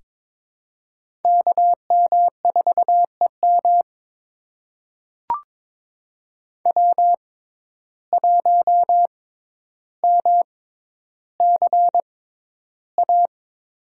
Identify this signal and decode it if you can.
Morse code, heard as a keyed tone, about 22 wpm, spaced as for 11 wpm, 700 Hz. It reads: KM4EM W1MCA
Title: USA Call Signs - 6x Character Spacing - 22wpm